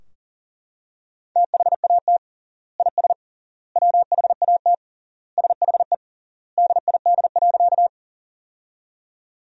Morse code, heard as a keyed tone, about 40 wpm, 700 Hz. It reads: THAT IS WHAT SHE DID.
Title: 3x Word-Spacing - Sentences from Top 100 Words - 40wpm